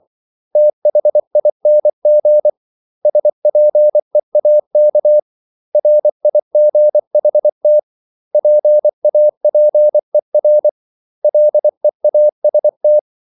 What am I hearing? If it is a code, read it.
Morse code, heard as a keyed tone, about 24 wpm, 600 Hz. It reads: THING SPEAK RIGHT PAPER LEAST
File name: CW / Practice / 500MostCommonEnglishWords24